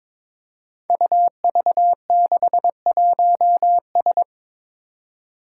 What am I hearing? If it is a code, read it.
Morse code, heard as a keyed tone, about 22 wpm, 700 Hz. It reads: UV61S
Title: Five Challenging Letter-Number Combinations 22wpm